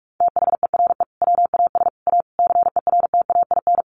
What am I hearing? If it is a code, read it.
Morse code, heard as a keyed tone, about 44 wpm, 700 Hz. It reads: THERE WAS A CERTAIN
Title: Sentences from Top 1200 Words 45wpm